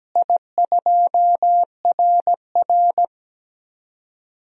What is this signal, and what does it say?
Morse code, heard as a keyed tone, about 17 wpm, 700 Hz. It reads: I2RR
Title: DXCC-100 Call Signs - 4 Characters - Repeated 3x - 17wpm